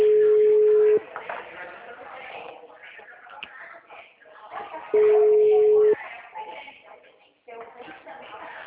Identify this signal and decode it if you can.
Ringback tone (Europe, 425 Hz), with speech.